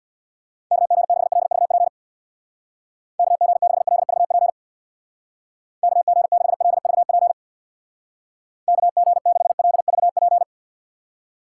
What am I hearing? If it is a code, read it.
Morse code, heard as a keyed tone, about 62 wpm, 700 Hz. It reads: KG6LVP KG6LVP KG6LVP KG6LVP
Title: Speed Racing - USA 2x3 Call Signs 50wpm